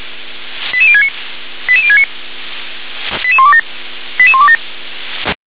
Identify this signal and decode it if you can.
Five-tone selective call (ZVEI), heard as quick runs of five tones, tones of about 70 ms. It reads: 88069 88069 88127 88127